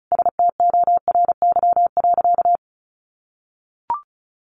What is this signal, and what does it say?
Morse code, heard as a keyed tone, about 34 wpm, 700 Hz. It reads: STORY.
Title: Sentences from Top 600 Words 35wpm